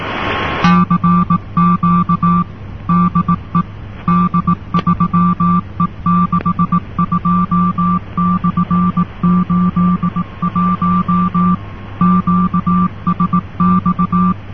Morse code, heard as a keyed tone, about 18 wpm, 180 Hz. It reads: CQ DE D3E62/81 QSX